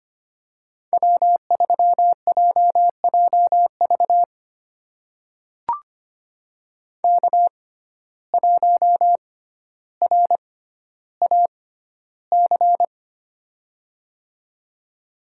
Morse code, heard as a keyed tone, about 25 wpm, spaced as for 13 wpm, 700 Hz. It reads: W3JJV K1RAC
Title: USA Call Signs - 5 Characters - 6x Character Spacing - 25wpm